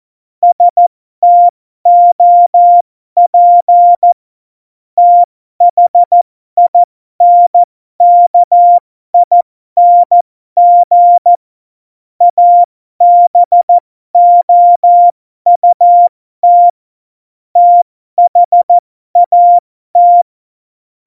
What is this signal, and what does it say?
Morse code, heard as a keyed tone, about 14 wpm, spaced as for 12 wpm, 700 Hz. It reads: STOP THINKING ABOUT THAT
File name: Morsekurs / Witze